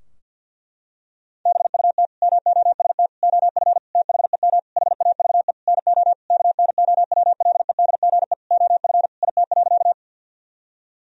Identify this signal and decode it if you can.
Morse code, heard as a keyed tone, about 50 wpm, 700 Hz. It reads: BUT MOST OF THEM HAVE NO KNOWLEDGE OF IT.
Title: Sentences from Top 1000 Words - Review Part 3 - 50wpm